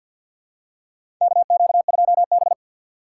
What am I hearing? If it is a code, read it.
Morse code, heard as a keyed tone, about 53 wpm, 700 Hz. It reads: KQ2B
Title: USA Call Signs - 4 Characters - 6x Character Spacing - 50wpm